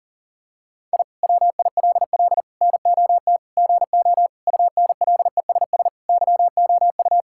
Morse code, heard as a keyed tone, about 40 wpm, 700 Hz. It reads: I WILL NOT GO UNLESS YOU
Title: Sentences from Top 1000 Words - Review Part 6 - 40wpm